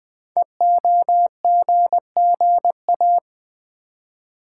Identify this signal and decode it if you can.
Morse code, heard as a keyed tone, about 20 wpm, 700 Hz. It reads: EOGGA
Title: Code Groups - 5 Letters - 20wpm